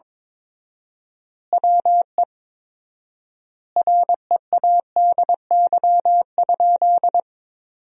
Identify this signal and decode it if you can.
Morse code, heard as a keyed tone, about 21 wpm, 700 Hz. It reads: WE READY?
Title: 4x Word-Spacing - Sentences from Top 400 Words - 22wpm